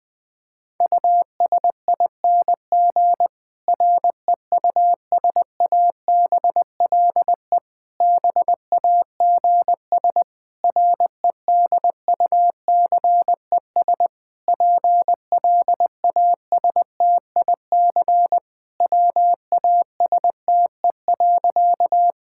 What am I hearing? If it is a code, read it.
Morse code, heard as a keyed tone, about 20 wpm, 700 Hz. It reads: USING REUSABLE BAGS REDUCES PLASTIC WASTE.
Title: Difficult Sentences with Hints - 20wpm